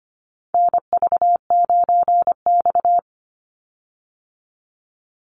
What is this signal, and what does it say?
Morse code, heard as a keyed tone, about 25 wpm, 700 Hz. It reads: NV9X
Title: Call Signs Repeated 2x - 25wpm